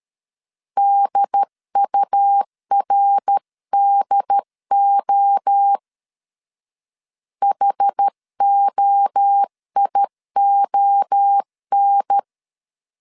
Morse code, heard as a keyed tone, about 13 wpm, 800 Hz. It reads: DURDO HOION
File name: CW / DET / 1 3 1500C S A 12wpm 2905